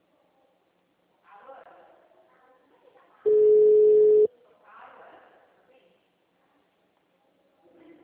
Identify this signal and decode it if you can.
Ringback tone (Europe, 425 Hz), with speech.